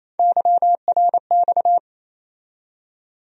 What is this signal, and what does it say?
Morse code, heard as a keyed tone, about 28 wpm, 700 Hz. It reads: YRX
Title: USA Call Sign Postfixes - 3 Letters 28wpm